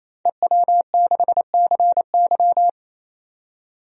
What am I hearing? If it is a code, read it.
Morse code, heard as a keyed tone, about 28 wpm, 700 Hz. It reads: EW6CY
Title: DXCC-100 Call Signs - 5 Characters - 28wpm